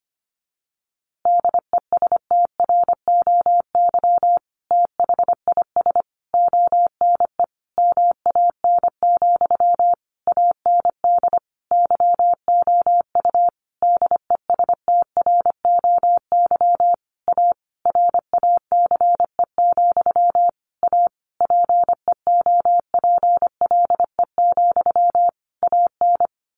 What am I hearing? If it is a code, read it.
Morse code, heard as a keyed tone, about 25 wpm, 700 Hz. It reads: DESTROY THIS ONE MAN, AND YOU DESTROY A RACE, A PEOPLE, AN
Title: A Sound Of Thunder 25wpm